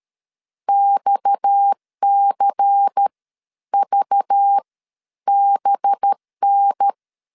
Morse code, heard as a keyed tone, about 13 wpm, spaced as for 10 wpm, 800 Hz. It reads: XCVBN
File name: CW / DET / 1 6 1500C S A 12wpm 3440